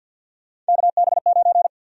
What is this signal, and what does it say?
Morse code, heard as a keyed tone, about 45 wpm, 700 Hz. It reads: KB9
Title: USA Call Sign Prefixes - 50wpm